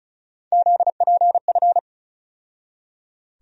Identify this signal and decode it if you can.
Morse code, heard as a keyed tone, about 36 wpm, 700 Hz. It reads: ZPF
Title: USA Call Sign Postfixes - 3 Letters 35wpm